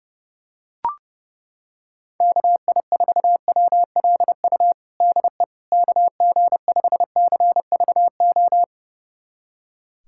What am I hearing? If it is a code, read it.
Morse code, heard as a keyed tone, about 30 wpm, 700 Hz. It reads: KI4WLU DE KG5CVO
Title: USA Callsign DE Callsign - 6 Characters - 30wpm